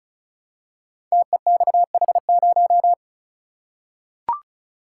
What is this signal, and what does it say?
Morse code, heard as a keyed tone, about 36 wpm, 700 Hz. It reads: TEXH0
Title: Code Groups - 5 Letter:Numbers - 35wpm